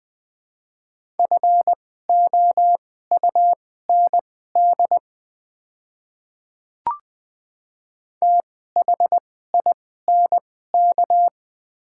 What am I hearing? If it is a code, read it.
Morse code, heard as a keyed tone, about 20 wpm, spaced as for 14 wpm, 700 Hz. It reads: FOUND THINK
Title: 2x Character Spacing - Top 300 Words - 20wpm